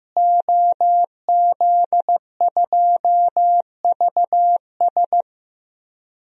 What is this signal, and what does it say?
Morse code, heard as a keyed tone, about 15 wpm, 700 Hz. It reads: OZ2VS
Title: DXCC-100 Call Signs - 5 Characters - Repeated 2x - 15wpm